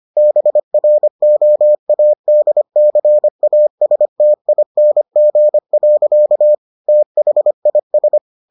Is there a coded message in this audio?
Yes: Morse code from 0.2 to 8.2 s, heard as a keyed tone, about 25 wpm, 600 Hz. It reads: BROADCASTING. THIS